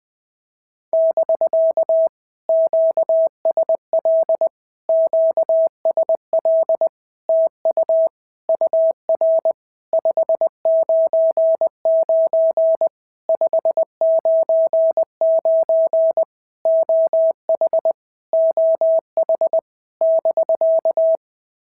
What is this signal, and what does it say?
Morse code, heard as a keyed tone, about 18 wpm, 650 Hz. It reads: <BK> QSL QSL TU UR 599 599 OH OH <BK>
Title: POTA QSO Practice - 20wpm